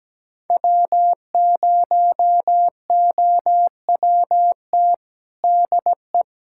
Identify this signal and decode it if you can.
Morse code, heard as a keyed tone, about 17 wpm, 700 Hz. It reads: W0OWT DE
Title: USA Callsign DE Callsign 17wpm